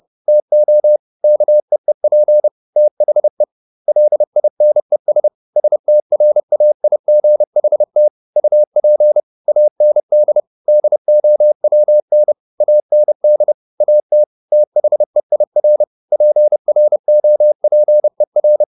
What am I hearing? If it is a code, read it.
Morse code, heard as a keyed tone, about 30 wpm, 600 Hz. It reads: TO KEEP THE LINES STRAIGHT UP AND DOWN AND AT THEIR PROPER
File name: the old man and the sea / the old man and the sea 1 30wpm